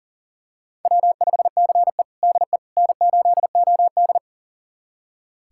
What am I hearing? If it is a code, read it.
Morse code, heard as a keyed tone, about 40 wpm, 700 Hz. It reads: W5CE DE N8OD
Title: USA Callsign DE Callsign - 4 Characters - 40wpm